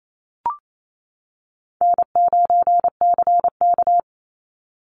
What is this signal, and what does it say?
Morse code, heard as a keyed tone, about 28 wpm, 700 Hz. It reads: N9CK
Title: Easy Call Signs - 28wpm